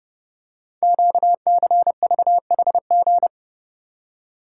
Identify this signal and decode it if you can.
Morse code, heard as a keyed tone, about 30 wpm, 700 Hz. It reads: QCVHG
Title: Code Groups - 5 Letters - 30wpm